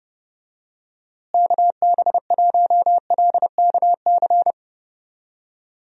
Morse code, heard as a keyed tone, about 30 wpm, 700 Hz. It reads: KB1LKC
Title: USA Call Signs - 6 Characters - Repeated 2x - 30wpm